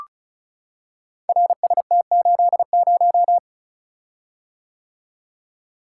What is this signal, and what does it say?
Morse code, heard as a keyed tone, about 36 wpm, 700 Hz. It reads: RST80
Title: Code Groups - 5 Letter:Numbers - 35wpm